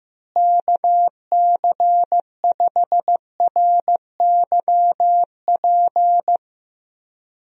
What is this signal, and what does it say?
Morse code, heard as a keyed tone, about 16 wpm, 700 Hz. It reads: KC5RYP
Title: USA Call Signs - 6 Characters - Repeated 2x - 15wpm